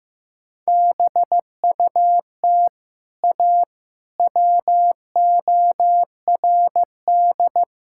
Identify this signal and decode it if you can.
Morse code, heard as a keyed tone, about 15 wpm, 700 Hz. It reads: BUT A WORD
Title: Sentences from Top 1600 Words 15wpm